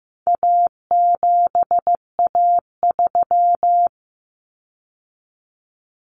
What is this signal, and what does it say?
Morse code, heard as a keyed tone, about 15 wpm, 700 Hz. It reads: A7A3
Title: Common Cut Numbers - 4-Digit 15wpm